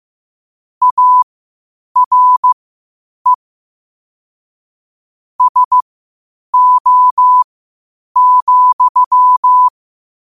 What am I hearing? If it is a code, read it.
Morse code, heard as a keyed tone, about 15 wpm, spaced as for 8 wpm, 1000 Hz. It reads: ARE SO,
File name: qotdfiles / 8wpm